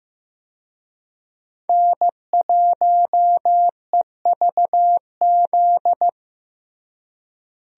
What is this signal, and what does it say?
Morse code, heard as a keyed tone, about 16 wpm, 700 Hz. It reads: N1EVZ
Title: USA Call Signs - 5 Characters - Repeated 3x - 15wpm